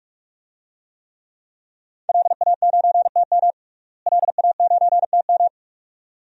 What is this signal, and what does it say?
Morse code, heard as a keyed tone, about 48 wpm, 700 Hz. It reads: LA9TM LA9TM
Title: DXCC-100 Call Signs - 5 Characters - Repeated 2x - 45wpm